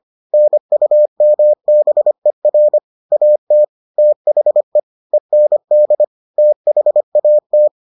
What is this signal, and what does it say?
Morse code, heard as a keyed tone, about 25 wpm, 600 Hz. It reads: NUMBER AT THE END THAT